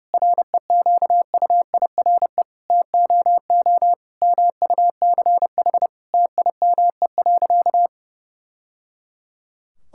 Morse code, heard as a keyed tone, about 30 wpm, 700 Hz. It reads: REQUIRE TOO MUCH TIME.
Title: Sentences from Top 1000 Words - Review Part 6 - 30wpm